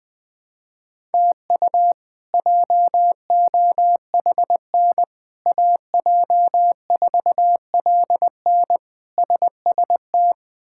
Morse code, heard as a keyed tone, about 21 wpm, 700 Hz. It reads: TU JOHN AJ4LN SST
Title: K1USN SST QSO Practice - 20wpm